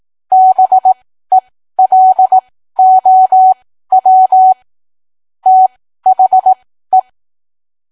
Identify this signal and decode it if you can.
Morse code, heard as a keyed tone, about 18 wpm, spaced as for 13 wpm, 750 Hz. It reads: BELOW THE